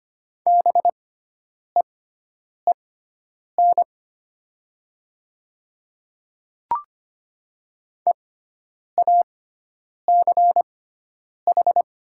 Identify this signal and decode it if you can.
Morse code, heard as a keyed tone, about 25 wpm, spaced as for 9 wpm, 700 Hz. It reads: BEEN EACH
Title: 6x Character Spacing - Top 100 Words - 25wpm